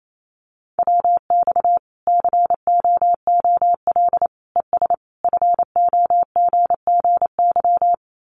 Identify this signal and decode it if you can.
Morse code, heard as a keyed tone, about 28 wpm, 700 Hz. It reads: WX COOL ES FOGGY
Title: Common QSO Phrases 28wpm